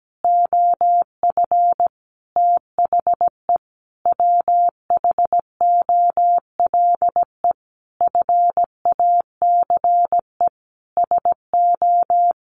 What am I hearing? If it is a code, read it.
Morse code, heard as a keyed tone, about 17 wpm, 700 Hz. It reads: OF THE WHOLE FACE SO
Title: Sentences from Top 1400 Words 17wpm